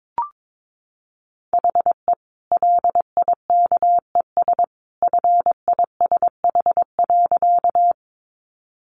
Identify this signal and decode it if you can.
Morse code, heard as a keyed tone, about 22 wpm, 700 Hz. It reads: HE LIKES FISH.
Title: Easy Phrases 22wpm